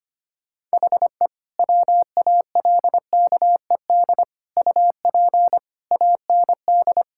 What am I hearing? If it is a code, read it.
Morse code, heard as a keyed tone, about 25 wpm, 700 Hz. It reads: HE WALKED UP AND
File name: practice / 25wpm / Sentences with 4-Word Phrases - 25wpm